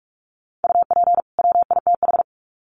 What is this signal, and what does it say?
Morse code, heard as a keyed tone, about 44 wpm, 700 Hz. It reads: UP WITH
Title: Sets of 2 Words 45wpm